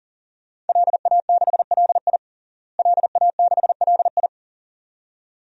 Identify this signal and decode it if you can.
Morse code, heard as a keyed tone, about 40 wpm, 700 Hz. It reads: LA6LI LA6LI
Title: DXCC-100 Call Signs - 5 Characters - Repeated 2x - 40wpm